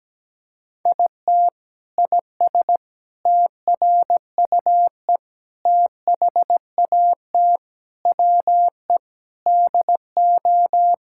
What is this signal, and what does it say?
Morse code, heard as a keyed tone, about 17 wpm, 700 Hz. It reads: IT IS TRUE THAT WE DO
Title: Sentences from Top 1600 Words 17wpm